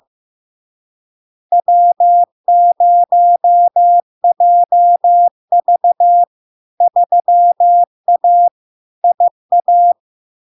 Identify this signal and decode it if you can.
Morse code, heard as a keyed tone, about 15 wpm, 700 Hz. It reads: W0JV 3A IA
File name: LARK-FD-2022-Log-in-CW-15wpm-700-Hz